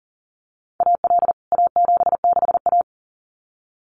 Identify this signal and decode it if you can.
Morse code, heard as a keyed tone, about 40 wpm, 700 Hz. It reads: AL A76A
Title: CWT Exchange 40wpm